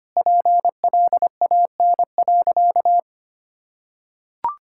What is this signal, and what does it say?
Morse code, heard as a keyed tone, about 25 wpm, 700 Hz. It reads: PLAN.
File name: practice / 25wpm / Sentences with 2-Word Phrases - 25wpm